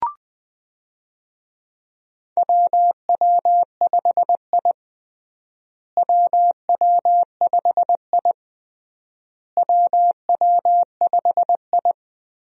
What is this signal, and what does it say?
Morse code, heard as a keyed tone, about 21 wpm, 700 Hz. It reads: WW5I WW5I WW5I
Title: USA Call Signs - 4 Characters - Repeated 3x - 20wpm